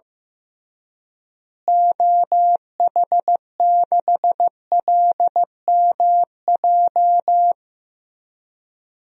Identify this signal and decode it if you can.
Morse code, heard as a keyed tone, about 15 wpm, 700 Hz. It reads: OH6LMJ
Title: DXCC-100 Call Signs - 6 Characters - Repeated 3x - 15wpm